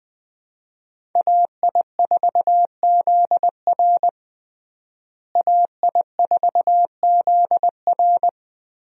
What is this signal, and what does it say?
Morse code, heard as a keyed tone, about 20 wpm, 700 Hz. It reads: AI4ZR AI4ZR
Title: USA Call Signs - 5 Characters - Repeated 3x - 20wpm